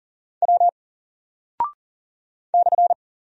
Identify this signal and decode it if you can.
Morse code, heard as a keyed tone, about 39 wpm, 700 Hz. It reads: W /
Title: Single Character (Letter:Number:Punctuation:Prosign) 40wpm